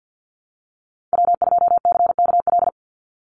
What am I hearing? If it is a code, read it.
Morse code, heard as a keyed tone, about 50 wpm, 700 Hz. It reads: W2CKL
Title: NA SOTA Activator Callsigns 50wpm